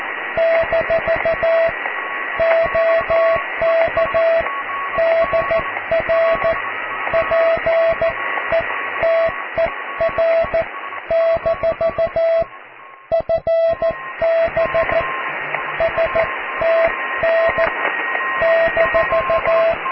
Morse code, heard as a keyed tone, about 14 wpm, spaced as for 12 wpm, 660 Hz, with radio noise. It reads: - OK DR PETER- FB STN -
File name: cw / QSO+QRM